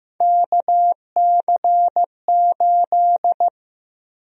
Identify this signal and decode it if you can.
Morse code, heard as a keyed tone, about 15 wpm, 700 Hz. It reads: KC8
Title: USA Call Sign Prefixes - Repeated 2x - 15wpm